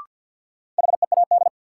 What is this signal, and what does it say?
Morse code, heard as a keyed tone, about 53 wpm, 700 Hz. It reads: HEAD
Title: Sentences from Top 300 Words - New Word Drill - 50wpm